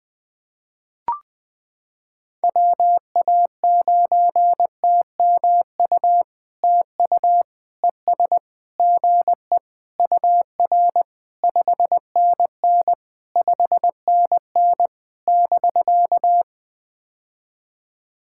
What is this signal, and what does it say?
Morse code, heard as a keyed tone, about 20 wpm, 700 Hz. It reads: WA9TMU TU ES GE UR 5NN 5NN <BK>
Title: POTA QSO Practice - 20wpm